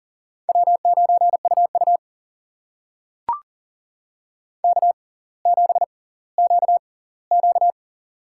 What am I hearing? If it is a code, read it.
Morse code, heard as a keyed tone, about 38 wpm, spaced as for 20 wpm, 700 Hz. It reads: W9UU K7QQ
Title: USA Call Signs - 4 Characters - 6x Character Spacing - 40wpm